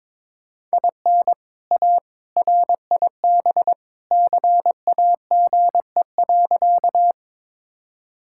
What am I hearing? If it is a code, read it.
Morse code, heard as a keyed tone, about 22 wpm, 700 Hz. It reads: IN A RIB CAGE.
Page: morse-code-ninja-practice-22wpm